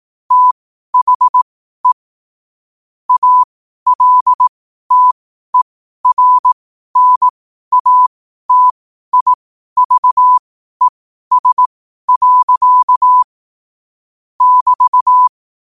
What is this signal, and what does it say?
Morse code, heard as a keyed tone, about 18 wpm, spaced as for 12 wpm, 1000 Hz. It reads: THE ALTERNATIVES. <BT>
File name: qotdfiles / 12wpm